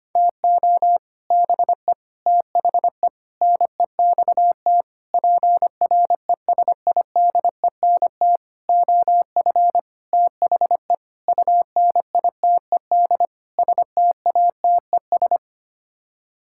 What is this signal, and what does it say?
Morse code, heard as a keyed tone, about 25 wpm, 700 Hz. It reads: TO BE THE NEXT PRESIDENT OF THE UNITED STATES
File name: Sets of 9 Words - 25wpm